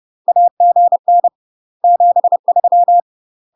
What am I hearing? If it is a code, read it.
Morse code, heard as a keyed tone, about 30 wpm, 700 Hz. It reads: AGN 73
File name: UK MASTER QSOs TRAINING / 30WPM / UK MASTER QSOs 0042